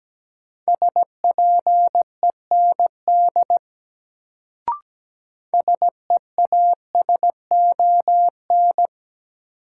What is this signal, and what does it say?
Morse code, heard as a keyed tone, about 17 wpm, 700 Hz. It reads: SPEND SEASON
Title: Sets of 5 Words - New Word Drill - 17wpm